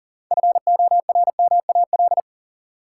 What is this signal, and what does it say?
Morse code, heard as a keyed tone, about 40 wpm, 700 Hz. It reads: FORMAL
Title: Sentences from Top 1300 Words - New Word Drill Alt - 40wpm